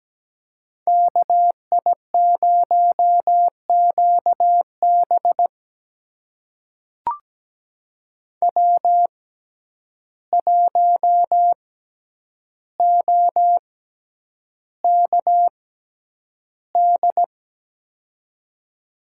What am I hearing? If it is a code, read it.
Morse code, heard as a keyed tone, about 17 wpm, spaced as for 9 wpm, 700 Hz. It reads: KI0QB W1OKD
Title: USA Call Signs - 5 Characters - 6x Character Spacing - 17wpm